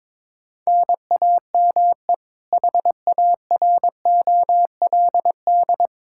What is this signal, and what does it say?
Morse code, heard as a keyed tone, about 22 wpm, 700 Hz. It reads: NAME HAROLD
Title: Practice QSO - Exchange 1 - Beginning - 22wpm